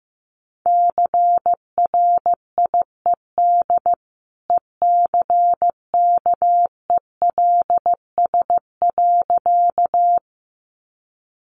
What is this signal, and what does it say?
Morse code, heard as a keyed tone, about 15 wpm, 700 Hz. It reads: CRIED ECKELS.
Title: A Sound Of Thunder 15wpm